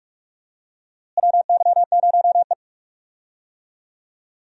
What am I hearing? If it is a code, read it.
Morse code, heard as a keyed tone, about 42 wpm, 700 Hz. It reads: WY0E
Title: USA Call Signs - 4 Characters - Repeated 3x - 45wpm